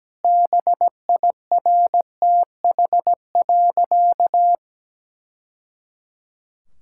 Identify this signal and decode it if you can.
Morse code, heard as a keyed tone, about 17 wpm, 700 Hz. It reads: BIRTH.
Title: Sentences from Top 1000 Words - Review Part 6 - 17wpm